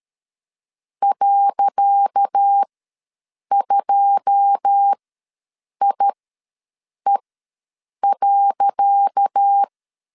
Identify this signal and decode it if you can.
Morse code, heard as a keyed tone, about 13 wpm, spaced as for 7 wpm, 800 Hz. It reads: .2IE.
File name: LEZIONI / 10 SIGNS / 10C DOT ws 12wpm 3548